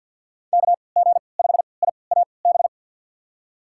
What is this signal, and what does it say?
Morse code, heard as a keyed tone, about 49 wpm, spaced as for 28 wpm, 700 Hz. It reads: KG5IAB
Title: USA Call Signs - 6 Characters - 3x Character Spacing - 50wpm